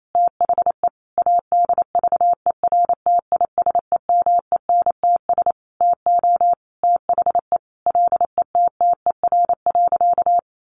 Morse code, heard as a keyed tone, about 28 wpm, 700 Hz. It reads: THE ADVERTISEMENTS TO THE LETTER.
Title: A Sound Of Thunder 28wpm